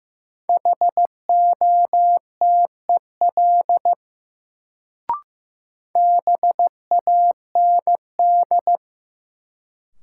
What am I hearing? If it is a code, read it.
Morse code, heard as a keyed tone, about 15 wpm, 700 Hz. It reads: HOTEL BAND
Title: Sentences from Top 1600 Words - New Word Drill Alt - 15wpm